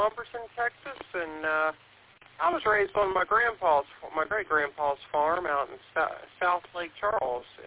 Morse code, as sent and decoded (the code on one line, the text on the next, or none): none